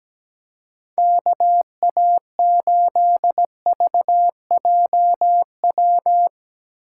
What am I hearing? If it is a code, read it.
Morse code, heard as a keyed tone, about 17 wpm, 700 Hz. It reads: KA8VJW
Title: USA Call Signs - 3x Character Spacing - 17wpm